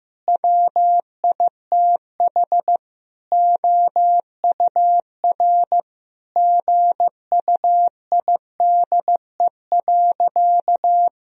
Morse code, heard as a keyed tone, about 15 wpm, 700 Hz. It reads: WITH OUR GUIDE.
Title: Sentences from Top 1000 Words - Review Part 3 - 15wpm